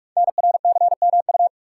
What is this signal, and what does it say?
Morse code, heard as a keyed tone, about 44 wpm, 700 Hz. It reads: NRCMU
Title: Code Groups - 5 Letters - 45wpm